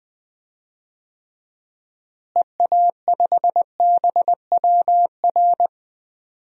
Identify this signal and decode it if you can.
Morse code, heard as a keyed tone, about 20 wpm, 700 Hz. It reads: EA5BWR
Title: DXCC-100 Call Signs - 6 Characters - Repeated 3x - 20wpm